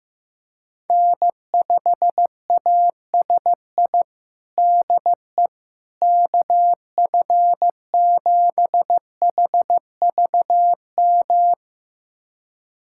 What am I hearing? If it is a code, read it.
Morse code, heard as a keyed tone, about 15 wpm, 700 Hz. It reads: N5ASI DE KF7HVM
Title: USA Callsign DE Callsign 15wpm